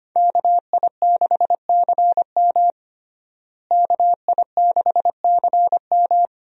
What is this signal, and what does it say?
Morse code, heard as a keyed tone, about 25 wpm, 700 Hz. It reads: KI6CM KI6CM
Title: USA Call Signs - 5 Characters - Repeated 3x - 25wpm